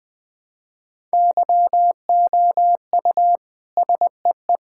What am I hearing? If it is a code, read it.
Morse code, heard as a keyed tone, about 21 wpm, 700 Hz. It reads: YOU SEE
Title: Sentences from Top 1000 Words - Review Part 6 - 20wpm